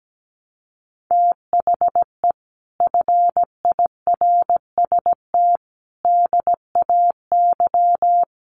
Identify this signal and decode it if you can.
Morse code, heard as a keyed tone, about 17 wpm, 700 Hz. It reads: THE FIRST DAY
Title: Sets of 5 Words Encore2 17wpm